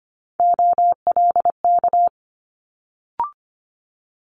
Morse code, heard as a keyed tone, about 25 wpm, 700 Hz. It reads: OLK
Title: Multi-letter Phonemes Quick Fire 25wpm